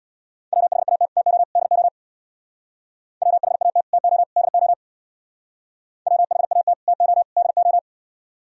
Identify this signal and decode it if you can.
Morse code, heard as a keyed tone, about 65 wpm, 700 Hz. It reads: WHAT TO DO WHAT TO DO WHAT TO DO